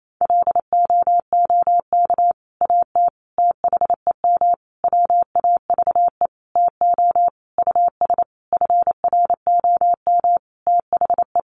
Morse code, heard as a keyed tone, about 28 wpm, 700 Hz. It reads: LOOK AT THEM WAVE TO US FROM THE
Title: Sentences from Top 1400 Words 28wpm